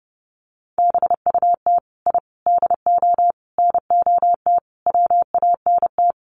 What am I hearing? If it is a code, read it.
Morse code, heard as a keyed tone, about 30 wpm, 700 Hz. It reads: BUT I DO NOT WANT
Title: Sets of 5 Words 30wpm